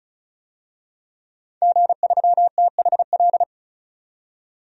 Morse code, heard as a keyed tone, about 34 wpm, 700 Hz. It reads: G3THL
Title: DXCC-100 Call Signs - 5 Characters - 35wpm